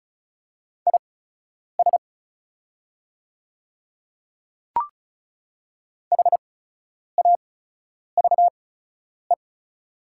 Morse code, heard as a keyed tone, about 35 wpm, spaced as for 10 wpm, 690 Hz. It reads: IS HAVE
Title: 8x Character Spacing - Top 100 Words - 35wpm